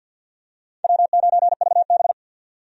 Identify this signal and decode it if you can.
Morse code, heard as a keyed tone, about 52 wpm, 700 Hz. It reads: W9VB
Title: USA Call Signs - 4 Characters - 50wpm